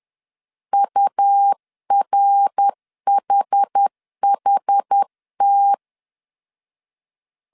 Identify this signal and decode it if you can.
Morse code, heard as a keyed tone, about 10 wpm, 800 Hz. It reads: URHHT
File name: 10wpm